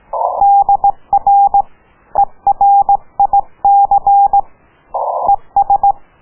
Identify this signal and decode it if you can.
Morse code, heard as a keyed tone, about 18 wpm, 800 Hz. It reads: DR ERIC ES